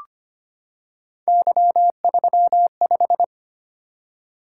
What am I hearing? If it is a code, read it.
Morse code, heard as a keyed tone, about 25 wpm, 700 Hz. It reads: Y35
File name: Three Letter-Number Combinations - 25wpm